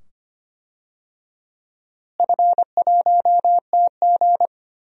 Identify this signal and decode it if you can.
Morse code, heard as a keyed tone, about 25 wpm, 700 Hz. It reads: F1TG
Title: DXCC-100 Call Signs - 4 Characters - Repeated 2x - 25wpm